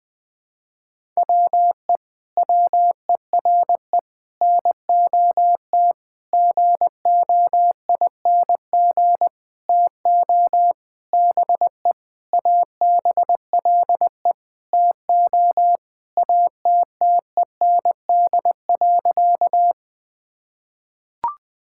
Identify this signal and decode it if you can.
Morse code, heard as a keyed tone, about 21 wpm, 700 Hz. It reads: WE WERE NOT GOING TO BE ABLE TO ATTEND.
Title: Sentences with 7-Word Phrases - 20wpm